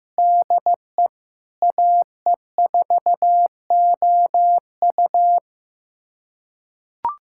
Morse code, heard as a keyed tone, about 16 wpm, 700 Hz. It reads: DE AE4OU
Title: USA Callsign DE Callsign - 5 Characters - 15wpm